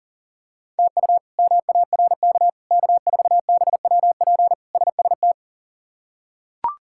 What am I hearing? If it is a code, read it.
Morse code, heard as a keyed tone, about 37 wpm, 700 Hz. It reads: TU MARK K4BWP SST
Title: K1USN SST QSO Practice - 40wpm